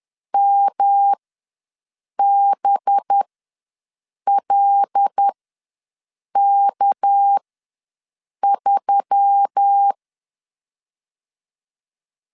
Morse code, heard as a keyed tone, about 10 wpm, spaced as for 6 wpm, 800 Hz. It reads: MBLK3
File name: LEZIONI / 6 PKX0 / 6 PK ws 10wpm 3745